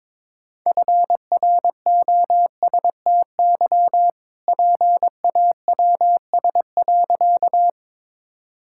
Morse code, heard as a keyed tone, about 22 wpm, 700 Hz. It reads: FROSTY PAWS.
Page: morse-code-ninja-practice-22wpm